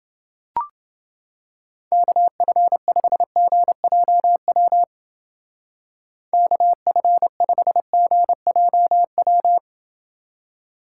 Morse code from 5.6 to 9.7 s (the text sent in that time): KF5GJW